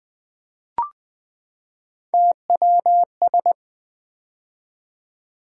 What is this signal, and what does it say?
Morse code, heard as a keyed tone, about 20 wpm, 700 Hz. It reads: TWS
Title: Three Letter-Number Combinations - 20wpm